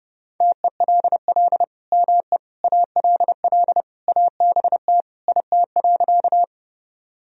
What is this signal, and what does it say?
Morse code, heard as a keyed tone, about 30 wpm, 700 Hz. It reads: TELL ME ALL ABT IT.
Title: Sentences from Top 200 Words and Abbreviations 30wpm